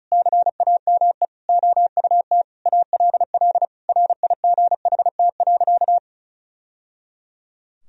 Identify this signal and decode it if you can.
Morse code, heard as a keyed tone, about 34 wpm, 700 Hz. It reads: CAME OUT ALL RIGHT.
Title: Sentences from Top 1000 Words - Review Part 4 - 35wpm